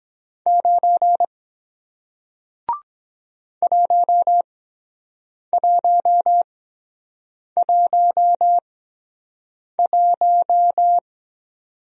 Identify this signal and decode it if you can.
Morse code, heard as a keyed tone, about 23 wpm, spaced as for 7 wpm, 700 Hz. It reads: 9 1111